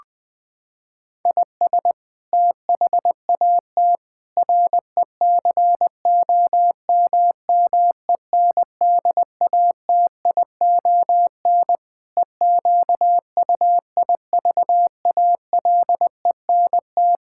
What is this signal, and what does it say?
Morse code, heard as a keyed tone, about 20 wpm, 700 Hz. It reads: IS THAT RECOMMENDATION EQUIVALENT
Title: Sentences from Top 1600 Words 20wpm